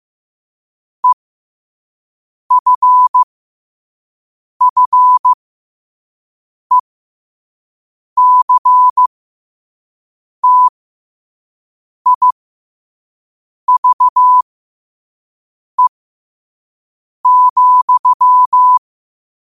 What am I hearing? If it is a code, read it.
Morse code, heard as a keyed tone, about 15 wpm, spaced as for 5 wpm, 1000 Hz. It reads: EFFECTIVE,
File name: qotdfiles / 5wpm